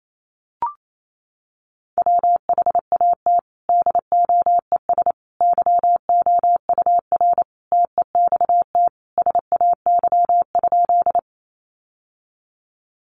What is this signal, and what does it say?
Morse code, heard as a keyed tone, about 28 wpm, 700 Hz. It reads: WHAT DOES YOUR TEXT SAY?
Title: Sentences from Top 1200 Words 28wpm